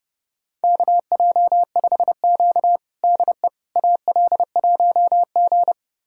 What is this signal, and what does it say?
Morse code, heard as a keyed tone, about 30 wpm, 700 Hz. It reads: KJ5Q DE AL1G
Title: USA Callsign DE Callsign - 4 Characters - 30wpm